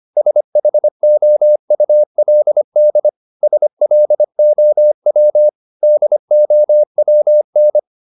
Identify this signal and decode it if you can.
Morse code, heard as a keyed tone, about 25 wpm, 600 Hz. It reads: SHOULD SLOW DOWN